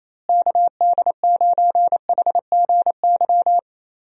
Morse code, heard as a keyed tone, about 27 wpm, 700 Hz. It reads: KD9HGY
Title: USA Call Signs - 6 Characters - Repeated 2x - 28wpm